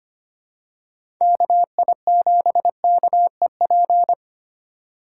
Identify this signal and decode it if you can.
Morse code, heard as a keyed tone, about 25 wpm, 700 Hz. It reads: KI7KEP
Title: USA Call Signs - 6 Characters - 3x Character Spacing - 25wpm